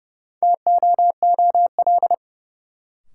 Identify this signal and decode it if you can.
Morse code, heard as a keyed tone, about 30 wpm, 700 Hz. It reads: TOOL
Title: Sentences from Top 1000 Words - Review Part 6 - New Word Drill - 30wpm